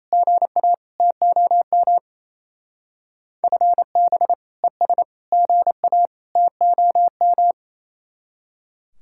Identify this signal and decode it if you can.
Morse code, heard as a keyed tone, about 35 wpm, 700 Hz. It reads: GA TOM FB ES GA TOM
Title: Speed-Racing Common QSO Phrases - 28wpm